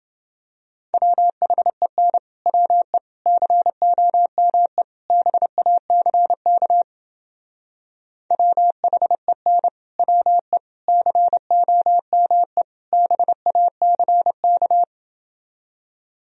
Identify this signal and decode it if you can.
Morse code, heard as a keyed tone, about 30 wpm, 700 Hz. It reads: WHEN WE COME BACK WHEN WE COME BACK